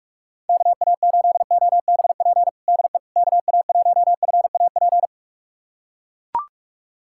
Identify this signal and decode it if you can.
Morse code, heard as a keyed tone, about 44 wpm, 700 Hz. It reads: KA8OBP DE KA1FAP
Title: USA Callsign DE Callsign - 6 Characters - 45wpm